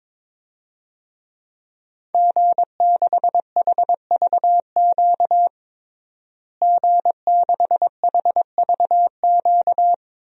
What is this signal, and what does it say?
Morse code, heard as a keyed tone, about 22 wpm, 700 Hz. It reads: G6HVQ G6HVQ
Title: DXCC-100 Call Signs - 5 Characters - Repeated 3x - 22wpm